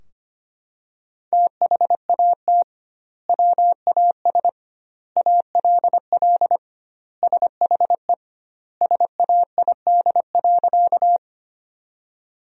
Morse code, heard as a keyed tone, about 25 wpm, 700 Hz. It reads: THAT WAS ALL SHE SAID.
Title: 2x Word-Spacing - Sentences from Top 100 Words - 25wpm